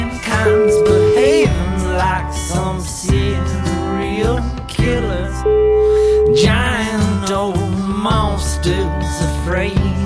Ringback tone (Europe, 425 Hz), with noise.